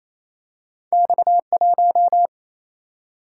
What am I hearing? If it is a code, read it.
Morse code, heard as a keyed tone, about 28 wpm, 700 Hz. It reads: X 1
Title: Single-Letter Single-Number 28wpm